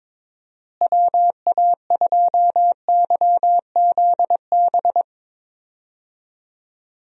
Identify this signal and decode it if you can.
Morse code, heard as a keyed tone, about 22 wpm, 700 Hz. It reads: WA2YZB
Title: USA Call Signs - 6 Characters - Repeated 3x - 22wpm